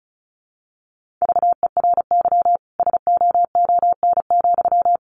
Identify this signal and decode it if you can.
Morse code, heard as a keyed tone, about 34 wpm, 700 Hz. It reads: VERY SOON,
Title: Sentences from Top 1400 Words 35wpm